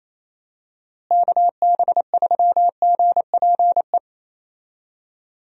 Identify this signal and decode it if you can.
Morse code, heard as a keyed tone, about 27 wpm, 700 Hz. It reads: KB3GPE
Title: USA Call Signs - 6 Characters - Repeated 2x - 28wpm